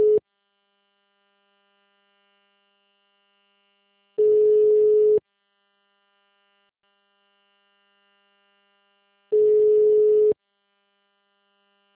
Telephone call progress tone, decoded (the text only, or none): ringback tone